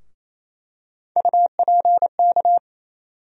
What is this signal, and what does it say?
Morse code, heard as a keyed tone, about 28 wpm, 700 Hz. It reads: UPK
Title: USA Call Sign Postfixes - 3 Letters 28wpm